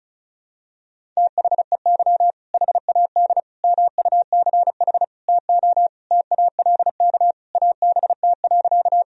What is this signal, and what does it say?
Morse code, heard as a keyed tone, about 36 wpm, 700 Hz. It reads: THEY HAD MUCH TO TALK ABT.
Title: Sentences from Top 400 Words and Abbreviations 35wpm